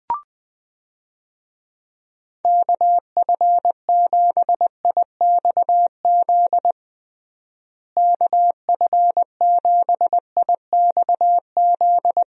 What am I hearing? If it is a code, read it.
Morse code, heard as a keyed tone, about 20 wpm, 700 Hz. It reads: KF7IXZ KF7IXZ
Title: USA Call Signs - 6 Characters - Repeated 2x - 20wpm